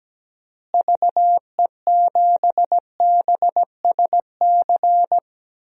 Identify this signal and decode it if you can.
Morse code, heard as a keyed tone, about 17 wpm, 700 Hz. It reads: VE7BSC
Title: DXCC-100 Call Signs - 6 Characters - Repeated 2x - 17wpm